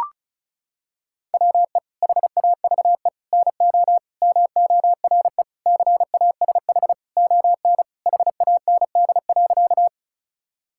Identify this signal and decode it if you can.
Morse code, heard as a keyed tone, about 34 wpm, 690 Hz. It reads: WE HAVE NO MORE CASH ON HAND.
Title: Sentences from Top 1000 Words - Review Part 5 - 35wpm